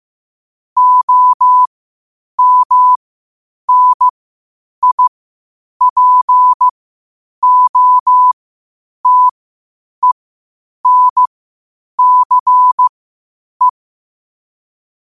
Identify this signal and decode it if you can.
Morse code, heard as a keyed tone, about 15 wpm, spaced as for 8 wpm, 1000 Hz. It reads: OMNIPOTENCE